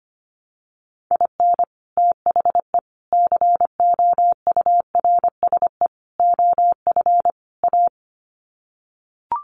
Morse code, heard as a keyed tone, about 25 wpm, 700 Hz. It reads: IN THE COURSE OF A
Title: Sets of 5 Words Encore2 25wpm